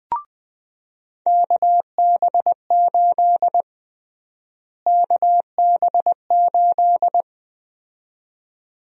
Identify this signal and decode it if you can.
Morse code, heard as a keyed tone, about 20 wpm, 700 Hz. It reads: KB8 KB8
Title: USA Call Sign Prefixes - Repeated 2x - 20wpm